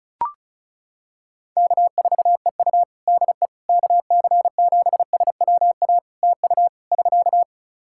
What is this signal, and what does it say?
Morse code, heard as a keyed tone, about 36 wpm, 700 Hz. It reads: K4EU DE KC7SWA TU <SK>
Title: Practice QSO - Exchange 4 - Ending - 35wpm